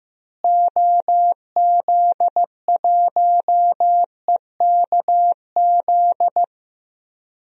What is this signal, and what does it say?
Morse code, heard as a keyed tone, about 15 wpm, 700 Hz. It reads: OZ1EKZ
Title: DXCC-100 Call Signs - 6 Characters - Repeated 3x - 15wpm